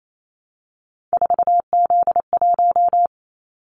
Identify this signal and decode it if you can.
Morse code, heard as a keyed tone, about 28 wpm, 700 Hz. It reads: 4Z1
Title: Callsign Prefixes for Top 50 DXCC - 28wpm